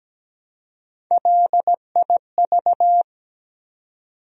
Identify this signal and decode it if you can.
Morse code, heard as a keyed tone, about 17 wpm, 700 Hz. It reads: LIV
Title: USA Common Names - Max 3 Letters - 17wpm